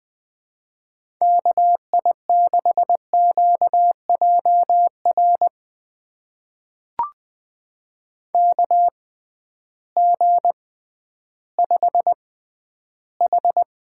Morse code, heard as a keyed tone, about 20 wpm, spaced as for 12 wpm, 700 Hz. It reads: KI6QJR KG5H
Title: USA Call Signs - 6 Characters - 6x Character Spacing - 20wpm